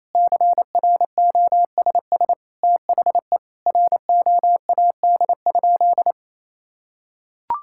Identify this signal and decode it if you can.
Morse code, heard as a keyed tone, about 28 wpm, 700 Hz. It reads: CROSS THE ROAD?
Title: Sentences from Top 1000 Words - Review Part 5 - 28wpm